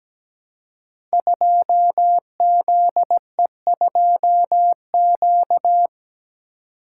Morse code, heard as a keyed tone, about 17 wpm, 700 Hz. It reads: 2ZE2Q
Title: Code Groups - 5 Letter:Numbers - 17wpm